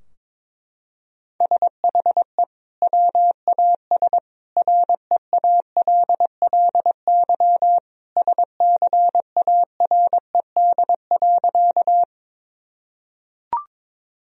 Morse code, heard as a keyed tone, about 22 wpm, 700 Hz. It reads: SHE WAS REALLY SCARED.
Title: Sentences from Top 1000 Words - Review Part 4 - 22wpm